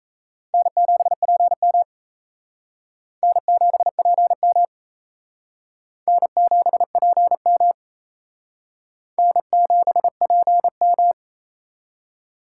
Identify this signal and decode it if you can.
Morse code, heard as a keyed tone, about 35 wpm, 700 Hz. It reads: N7PM N7PM N7PM N7PM
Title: Speed Racing - USA 1x2 Call Signs 28wpm